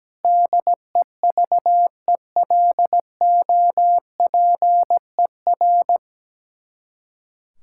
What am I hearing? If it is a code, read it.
Morse code, heard as a keyed tone, about 17 wpm, 700 Hz. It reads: DEVELOPER
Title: Practice QSO - Exchange 3 - More Info - 17wpm